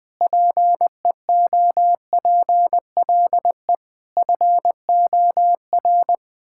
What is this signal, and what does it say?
Morse code, heard as a keyed tone, about 21 wpm, 700 Hz. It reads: PEOPLE FOR
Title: Sentences with 9-Word Phrases - 20wpm